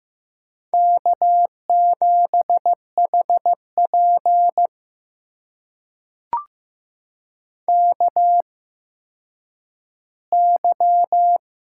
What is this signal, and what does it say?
Morse code, heard as a keyed tone, about 15 wpm, spaced as for 9 wpm, 700 Hz. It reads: K7HP KY
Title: USA Call Signs - 4 Characters - 8x Character Spacing - 15wpm